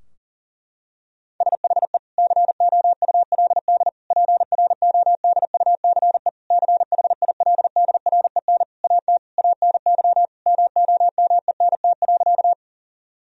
Morse code, heard as a keyed tone, about 40 wpm, 700 Hz. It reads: SHE COULD PRODUCE CHILDREN AT ANY MOMENT.
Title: Sentences from Top 1000 Words - Review Part 3 - 40wpm